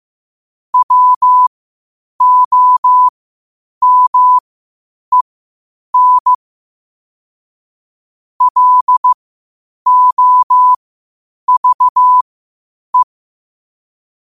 Morse code, heard as a keyed tone, about 15 wpm, spaced as for 8 wpm, 1000 Hz. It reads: WOMEN LOVE
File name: qotdfiles / 8wpm